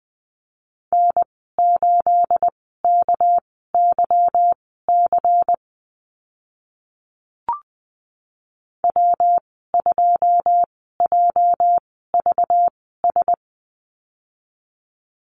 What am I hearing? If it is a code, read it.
Morse code, heard as a keyed tone, about 19 wpm, spaced as for 14 wpm, 700 Hz. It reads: N8KYC W2JVS
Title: 245 Lesson 39g - Letter z - Call Signs 20wpm